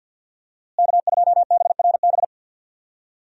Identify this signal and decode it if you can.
Morse code, heard as a keyed tone, about 52 wpm, 700 Hz. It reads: K2BRB
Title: USA Call Signs - 5 Characters - Repeated 2x - 50wpm